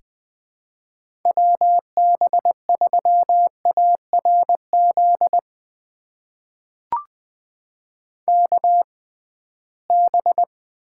Morse code, heard as a keyed tone, about 20 wpm, spaced as for 15 wpm, 700 Hz. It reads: WB3ARZ KB
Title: USA Call Signs - 6 Characters - 6x Character Spacing - 20wpm